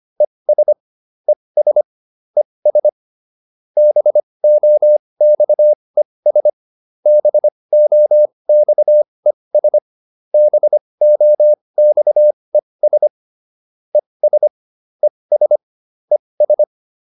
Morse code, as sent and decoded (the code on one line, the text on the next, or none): . ... / . ... / . ... / -... --- -..- . ... / -... --- -..- . ... / -... --- -..- . ... / . ... / . ... / . ...
ES ES ES BOXES BOXES BOXES ES ES ES